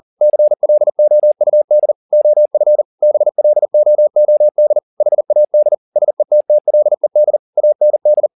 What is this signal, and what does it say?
Morse code, heard as a keyed tone, about 40 wpm, 600 Hz. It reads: CLOUD OF BLOOD HAD SETTLED AND
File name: the old man and the sea 2 40wpm